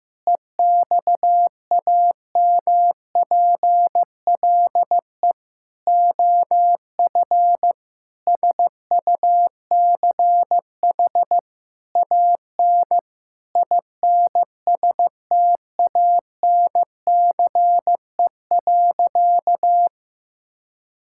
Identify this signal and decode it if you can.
Morse code, heard as a keyed tone, about 16 wpm, 700 Hz. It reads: EXAMPLE OF SUCH AN INSTANCE.
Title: Sentences from Top 1000 Words - Review Part 5 - 15wpm